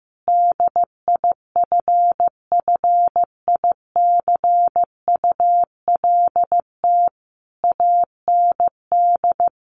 Morse code, heard as a keyed tone, about 15 wpm, 700 Hz. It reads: DIFFICULT AND
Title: Sentences from Top 1200 Words 15wpm